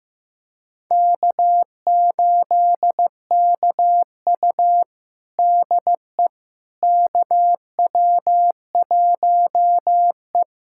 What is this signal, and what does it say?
Morse code, heard as a keyed tone, about 16 wpm, 700 Hz. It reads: K8KU DE KW1E
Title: USA Callsign DE Callsign - 4 Characters - 15wpm